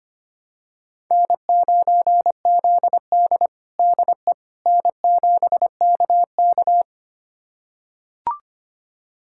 Morse code, heard as a keyed tone, about 25 wpm, 700 Hz. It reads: N9ZD DE N7KK
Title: USA Callsign DE Callsign - 4 Characters - 25wpm